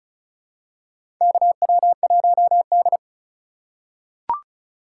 Morse code, heard as a keyed tone, about 34 wpm, 700 Hz. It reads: KW1D
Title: USA Call Signs - 4 Characters - 8x Character Spacing - 35wpm